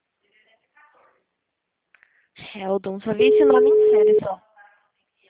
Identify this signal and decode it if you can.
Ringback tone (Europe, 425 Hz), with speech.